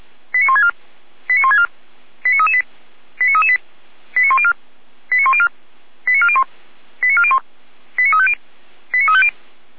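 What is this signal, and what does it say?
Five-tone selective call (ZVEI), heard as quick runs of five tones, tones of about 70 ms. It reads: 89164 89164 89338 89338 89104 89104 89401 89401 89350 89350
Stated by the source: Received by SDR; ZVEI five-tone code 89701.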